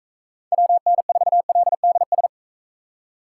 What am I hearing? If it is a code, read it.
Morse code, heard as a keyed tone, about 42 wpm, 700 Hz. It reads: WN4LDS